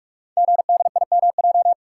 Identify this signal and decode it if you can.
Morse code, heard as a keyed tone, about 44 wpm, 700 Hz. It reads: GDIMJ
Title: Code Groups - 5 Letters - 45wpm